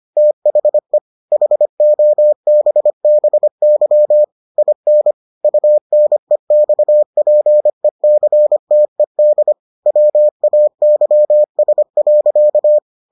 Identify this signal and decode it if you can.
Morse code, heard as a keyed tone, about 25 wpm, 600 Hz. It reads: THE HOBBY IN UNEXPECTED WAYS.